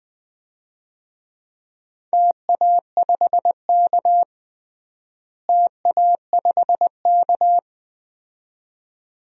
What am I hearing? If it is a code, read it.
Morse code, heard as a keyed tone, about 20 wpm, 700 Hz. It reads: TA5K TA5K
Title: DXCC-100 Call Signs - 4 Characters - Repeated 2x - 20wpm